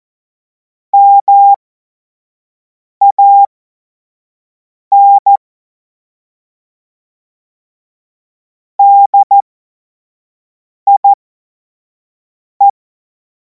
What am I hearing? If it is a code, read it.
Morse code, heard as a keyed tone, about 14 wpm, spaced as for 5 wpm, 800 Hz. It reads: MAN DIE